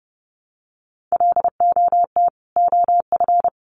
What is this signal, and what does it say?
Morse code, heard as a keyed tone, about 30 wpm, 700 Hz. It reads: LOT OF
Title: Sets of 2 Words 30wpm